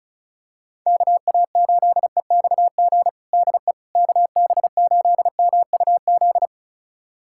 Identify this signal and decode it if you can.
Morse code, heard as a keyed tone, about 36 wpm, 700 Hz. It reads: KA8EXG DE KB8MUZ
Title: USA Callsign DE Callsign 35wpm